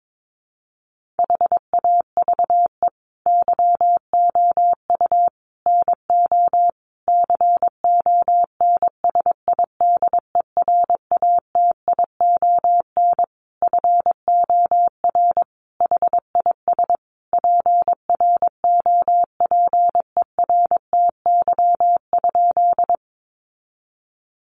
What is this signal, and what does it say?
Morse code, heard as a keyed tone, about 22 wpm, 700 Hz. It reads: HAVE YOU NO CONSIDERATION FOR HIS PROPERTY?
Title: Sentences from Top 1500 Words 22wpm